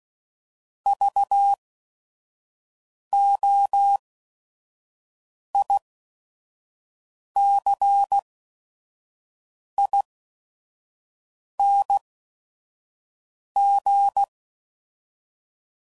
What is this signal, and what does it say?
Morse code, heard as a keyed tone, about 16 wpm, spaced as for 5 wpm, 800 Hz. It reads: VOICING
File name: Prefixes / By First Letter / Vs